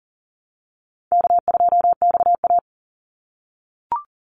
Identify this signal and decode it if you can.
Morse code, heard as a keyed tone, about 40 wpm, 700 Hz. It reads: K2XA
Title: Easy Call Signs - 40wpm